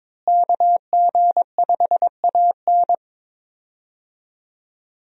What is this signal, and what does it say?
Morse code, heard as a keyed tone, about 22 wpm, 700 Hz. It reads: KG5AN
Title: USA Call Signs - 5 Characters - Repeated 2x - 22wpm